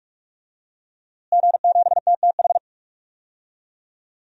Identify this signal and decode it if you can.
Morse code, heard as a keyed tone, about 46 wpm, 700 Hz. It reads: G7TTH
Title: DXCC-100 Call Signs - 5 Characters - Repeated 3x - 45wpm